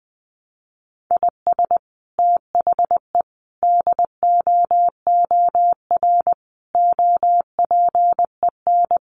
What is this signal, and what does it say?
Morse code, heard as a keyed tone, about 20 wpm, 700 Hz. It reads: IS THE DOOR OPEN
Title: Sentences from Top 1300 Words 20wpm